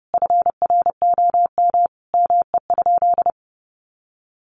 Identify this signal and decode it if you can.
Morse code, heard as a keyed tone, about 30 wpm, 700 Hz. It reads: FROM ME?
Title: Sentences from Top 1500 Words 30wpm